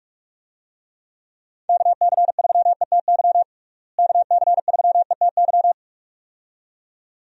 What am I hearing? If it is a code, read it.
Morse code, heard as a keyed tone, about 46 wpm, 700 Hz. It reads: KC3ETY KC3ETY
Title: USA Call Signs - 6 Characters - Repeated 2x - 45wpm